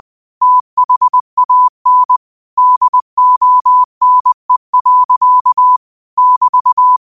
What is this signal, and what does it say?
Morse code, heard as a keyed tone, about 20 wpm, 1000 Hz. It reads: THAN DONE. <BT>
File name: qotdfiles / 20wpm